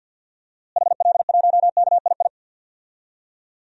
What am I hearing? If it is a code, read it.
Morse code, heard as a keyed tone, about 47 wpm, 700 Hz. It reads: HL1KII
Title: DXCC-100 Call Signs - 6 Characters - Repeated 2x - 50wpm